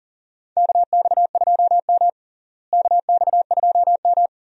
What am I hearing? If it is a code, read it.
Morse code, heard as a keyed tone, about 40 wpm, 700 Hz. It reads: KX2M KX2M
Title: USA Call Signs - 4 Characters - Repeated 2x - 40wpm